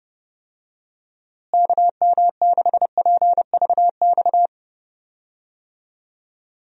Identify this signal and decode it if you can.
Morse code, heard as a keyed tone, about 30 wpm, 700 Hz. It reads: KM6PVX
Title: USA Call Signs - 6 Characters - Repeated 3x - 30wpm